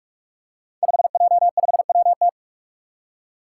Morse code, heard as a keyed tone, about 46 wpm, 700 Hz. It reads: 5J5WT
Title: Five Challenging Letter-Number Combinations 45wpm